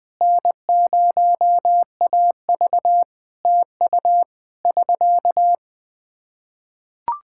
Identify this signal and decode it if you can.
Morse code, heard as a keyed tone, about 21 wpm, 700 Hz. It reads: N0AV TU <SK>
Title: Practice QSO - Exchange 4 - Ending - 20wpm